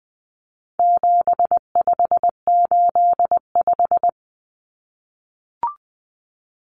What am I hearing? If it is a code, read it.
Morse code, heard as a keyed tone, about 20 wpm, 700 Hz. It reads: 7585
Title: Common Cut Numbers - 4-Digit 20wpm